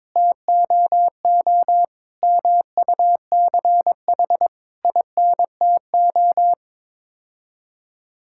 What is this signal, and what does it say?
Morse code, heard as a keyed tone, about 21 wpm, 700 Hz. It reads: TOO MUCH INTO
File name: Sets of 9 Words - 22wpm